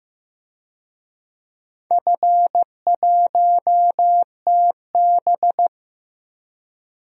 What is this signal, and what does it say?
Morse code, heard as a keyed tone, about 15 wpm, 700 Hz. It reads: F1TB
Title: DXCC-100 Call Signs - 4 Characters - 15wpm